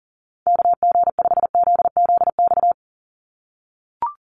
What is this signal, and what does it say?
Morse code, heard as a keyed tone, about 40 wpm, 700 Hz. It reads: KG5ZZX
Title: USA Call Signs 40wpm